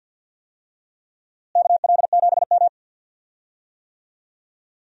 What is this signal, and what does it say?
Morse code, heard as a keyed tone, about 52 wpm, 700 Hz. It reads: KL7M
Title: DXCC-100 Call Signs - 4 Characters - Repeated 3x - 50wpm